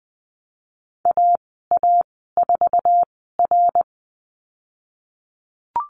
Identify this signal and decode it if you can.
Morse code, heard as a keyed tone, about 20 wpm, spaced as for 15 wpm, 700 Hz. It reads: AA4R
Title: 055 Lesson 10g - Letter r - Call Signs 20wpm